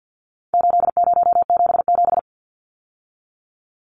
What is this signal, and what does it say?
Morse code, heard as a keyed tone, about 50 wpm, 700 Hz. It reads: 8077
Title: Numbers - 4-Digit 50wpm